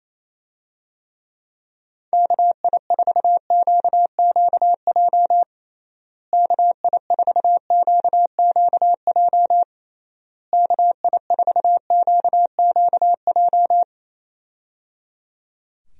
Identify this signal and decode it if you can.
Morse code, heard as a keyed tone, about 28 wpm, 700 Hz. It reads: KI4QQJ KI4QQJ KI4QQJ
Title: USA Call Signs - 6 Characters - Repeated 3x - 28wpm